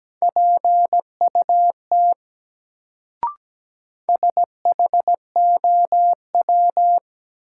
Morse code, heard as a keyed tone, about 17 wpm, 700 Hz. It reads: PUT SHOW
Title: Sentences from Top 200 Words - New Word Drill - 17wpm